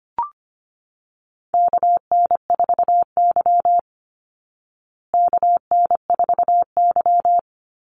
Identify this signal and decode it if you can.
Morse code, heard as a keyed tone, about 25 wpm, 700 Hz. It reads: KN4Y KN4Y
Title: Call Signs Repeated 3x - 25wpm